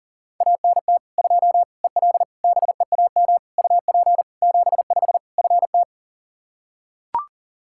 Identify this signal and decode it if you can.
Morse code, heard as a keyed tone, about 40 wpm, 700 Hz. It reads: ANT 2 EL BEAM UP 75 FT
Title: Practice QSO - Exchange 2 - Info - 40wpm